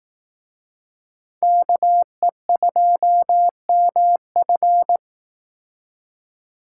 Morse code, heard as a keyed tone, about 18 wpm, 700 Hz. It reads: KE2MF
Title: Speed Racing - USA 2x2 Call Signs 15wpm